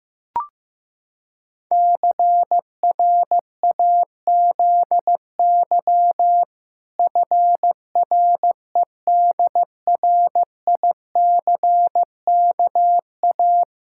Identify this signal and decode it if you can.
Morse code, heard as a keyed tone, about 16 wpm, 700 Hz. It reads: CRAZY FREDRICKA
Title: Pangram Challenge v2 - 15wpm